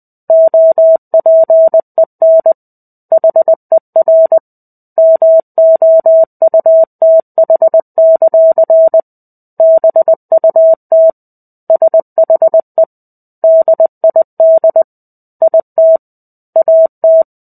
Morse code, heard as a keyed tone, about 20 wpm, 650 Hz. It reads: OPEN HER MOUTH; BUT SHE DID IT AT
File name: 05-Chapter-05 20 WPM